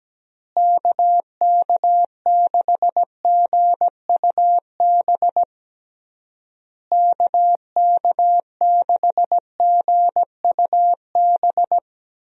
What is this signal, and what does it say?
Morse code, heard as a keyed tone, about 17 wpm, 700 Hz. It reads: KK6GUB KK6GUB
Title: USA Call Signs - 6 Characters - Repeated 3x - 17wpm